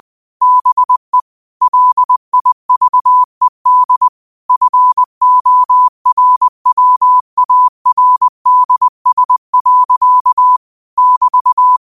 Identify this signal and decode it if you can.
Morse code, heard as a keyed tone, about 20 wpm, 1000 Hz. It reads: BE LIVED FORWARDS. <BT>
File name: qotdfiles / 20wpm